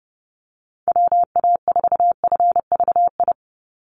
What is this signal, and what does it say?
Morse code, heard as a keyed tone, about 30 wpm, 700 Hz. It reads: WA4FVI
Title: USA Call Signs 30wpm